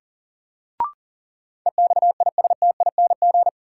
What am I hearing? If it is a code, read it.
Morse code, heard as a keyed tone, about 39 wpm, 700 Hz. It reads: EXISTING
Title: Sentences from Top 1500 Words - New Word Drill - 40wpm